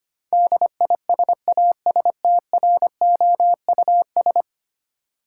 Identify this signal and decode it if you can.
Morse code, heard as a keyed tone, about 25 wpm, 700 Hz. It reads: DISASTROUS
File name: 10 Letter Words - First 3 Letters Given - 25wpm